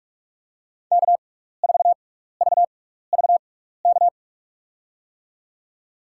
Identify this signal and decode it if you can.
Morse code, heard as a keyed tone, about 44 wpm, spaced as for 15 wpm, 700 Hz. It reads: K4VVK
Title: USA Call Signs - 5 Characters - 6x Character Spacing - 45wpm